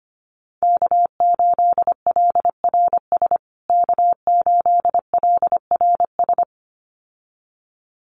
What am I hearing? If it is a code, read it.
Morse code, heard as a keyed tone, about 25 wpm, 700 Hz. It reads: K8LRS K8LRS
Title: USA Call Signs Repeated 2x 25wpm